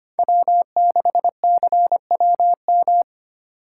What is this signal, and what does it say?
Morse code, heard as a keyed tone, about 25 wpm, 700 Hz. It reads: W6CWM
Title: K1USN SST QSO Practice - 25wpm